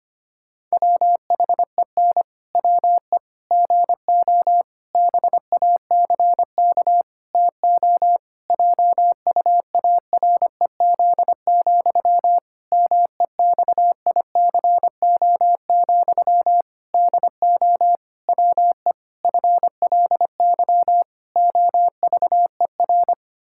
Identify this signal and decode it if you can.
Morse code, heard as a keyed tone, about 25 wpm, 700 Hz. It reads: WHEN WE GO BACK TO JUAREZ, MEXICO, DO WE FLY OVER
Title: Pangram Challenge v2 - 25wpm